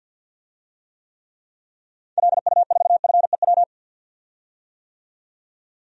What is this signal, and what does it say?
Morse code, heard as a keyed tone, about 51 wpm, 700 Hz. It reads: LU4FEW
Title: DXCC-100 Call Signs - 6 Characters - 50wpm